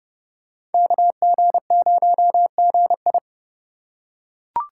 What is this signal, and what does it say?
Morse code, heard as a keyed tone, about 31 wpm, 700 Hz. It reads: KG0GI
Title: USA Call Signs - 5 Characters - 3x Character Spacing - 30wpm